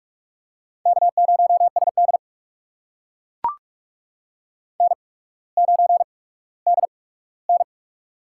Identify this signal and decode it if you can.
Morse code, heard as a keyed tone, about 44 wpm, spaced as for 19 wpm, 690 Hz. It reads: K0SD N9DN
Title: USA Call Signs - 4 Characters - 8x Character Spacing - 45wpm